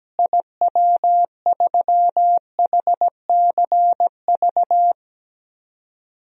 Morse code, heard as a keyed tone, about 17 wpm, 700 Hz. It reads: IW3HCV
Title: DXCC-100 Call Signs - 6 Characters - Repeated 3x - 17wpm